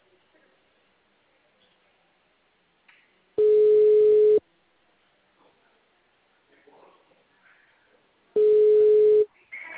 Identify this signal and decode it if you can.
Ringback tone (Europe, 425 Hz).